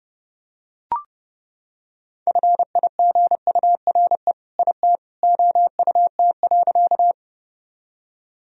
Morse code, heard as a keyed tone, about 32 wpm, 700 Hz. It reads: FIGURE IT OUT.
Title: Sentences from Top 1000 Words - Review Part 2 - 30wpm